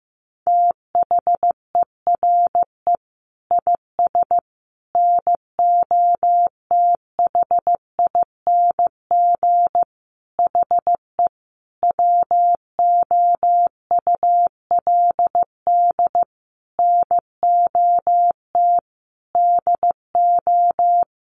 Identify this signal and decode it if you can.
Morse code, heard as a keyed tone, about 14 wpm, 700 Hz. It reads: THERE IS NOTHING HE WOULD NOT DO
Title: Sentences from Top 600 Words 15wpm